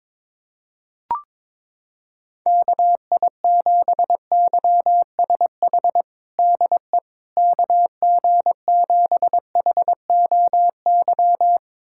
Morse code, heard as a keyed tone, about 22 wpm, 700 Hz. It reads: KI7YSH DE KG7HOY
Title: USA Callsign DE Callsign 22wpm